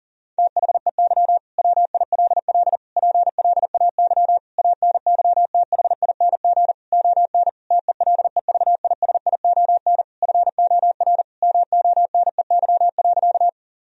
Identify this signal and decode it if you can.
Morse code, heard as a keyed tone, about 40 wpm, 700 Hz. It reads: THEY WILL PLAY ANYTHING ON TELEVISION FOR MONEY.
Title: Sentences from Top 1000 Words - Review Part 4 - 40wpm